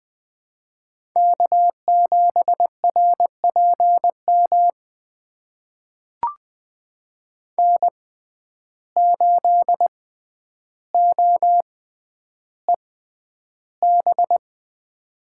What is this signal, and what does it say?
Morse code, heard as a keyed tone, about 20 wpm, spaced as for 10 wpm, 700 Hz. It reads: K7RPM N8OEB
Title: USA Call Signs - 5 Characters - 6x Character Spacing - 20wpm